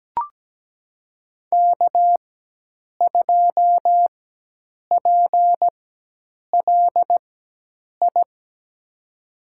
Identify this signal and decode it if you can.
Morse code, heard as a keyed tone, about 17 wpm, spaced as for 8 wpm, 700 Hz. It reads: K2PLI
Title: USA Call Signs - 5 Characters - 4x Character Spacing - 17wpm